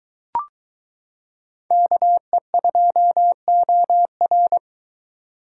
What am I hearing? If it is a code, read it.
Morse code, heard as a keyed tone, about 22 wpm, 700 Hz. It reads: KE2OR